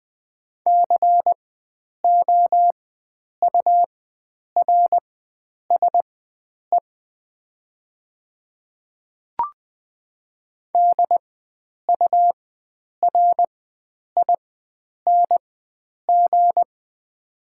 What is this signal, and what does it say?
Morse code, heard as a keyed tone, about 20 wpm, spaced as for 9 wpm, 700 Hz. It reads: COURSE DURING
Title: 4x Character Spacing - Top 500 Words - 20wpm